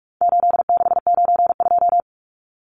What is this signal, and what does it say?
Morse code, heard as a keyed tone, about 44 wpm, 700 Hz. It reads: 8692
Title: Numbers - 4-Digit 45wpm